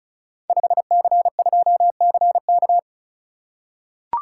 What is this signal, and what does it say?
Morse code, heard as a keyed tone, about 36 wpm, 700 Hz. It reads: 5C2CK
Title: Code Groups - 5 Letter:Numbers - 35wpm